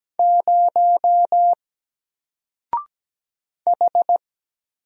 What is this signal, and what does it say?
Morse code, heard as a keyed tone, about 17 wpm, 700 Hz. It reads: 0 H
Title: Single Letter-Number Alt 17wpm